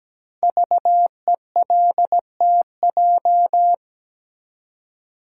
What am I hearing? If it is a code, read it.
Morse code, heard as a keyed tone, about 17 wpm, 700 Hz. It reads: VELTJ
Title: Code Groups - 5 Letter:Numbers - 17wpm